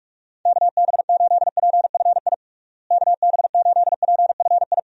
Morse code, heard as a keyed tone, about 41 wpm, 700 Hz. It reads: KB8PFI KB8PFI
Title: USA Call Signs - 6 Characters - Repeated 2x - 45wpm